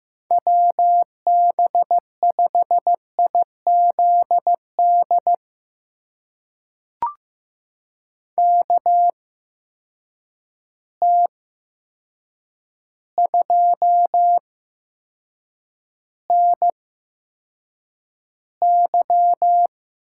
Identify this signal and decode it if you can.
Morse code, heard as a keyed tone, about 15 wpm, spaced as for 7 wpm, 700 Hz. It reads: WB5IZD KT2NY
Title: USA Call Signs - 6 Characters - 8x Character Spacing - 15wpm